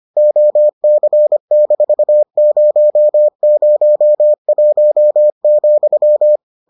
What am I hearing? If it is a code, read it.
Morse code, heard as a keyed tone, about 25 wpm, 600 Hz. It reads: OC-001,